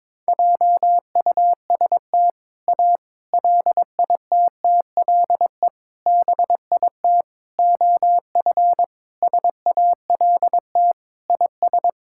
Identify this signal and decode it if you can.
Morse code, heard as a keyed tone, about 22 wpm, 700 Hz. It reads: JUST A LITTLE BIT OF SALT IS
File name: practice / 22wpm / Sentences with 4-Word Phrases - 22wpm